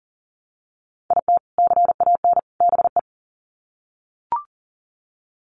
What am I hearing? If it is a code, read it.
Morse code, heard as a keyed tone, about 40 wpm, 700 Hz. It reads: IT CAN BE
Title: Sets of 3 Words 40wpm